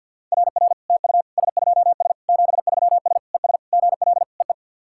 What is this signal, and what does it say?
Morse code, heard as a keyed tone, about 49 wpm, 690 Hz. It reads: RR TU S2S 73S ES GL EE
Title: SOTA QSO Practice 50wpm